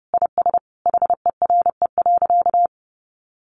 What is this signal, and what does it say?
Morse code, heard as a keyed tone, about 30 wpm, 700 Hz. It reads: IS HERE.
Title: Sentences from Top 600 Words 30wpm